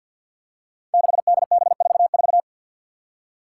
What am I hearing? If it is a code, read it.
Morse code, heard as a keyed tone, about 50 wpm, 700 Hz. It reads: 6DB44
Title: Five Challenging Letter-Number Combinations 50wpm